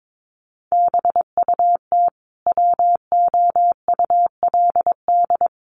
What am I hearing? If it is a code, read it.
Morse code, heard as a keyed tone, about 22 wpm, 700 Hz. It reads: BUT WOULD
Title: Sentences from Top 1400 Words 22wpm